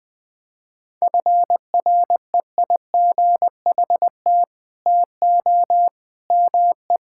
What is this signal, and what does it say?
Morse code, heard as a keyed tone, about 20 wpm, 700 Hz. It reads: FREIGHT TO ME
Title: Pangram Challenge v2 - 20wpm